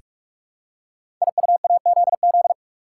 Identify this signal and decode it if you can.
Morse code, heard as a keyed tone, about 44 wpm, 700 Hz. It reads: IUAZZ
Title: Code Groups - 5 Letters - 45wpm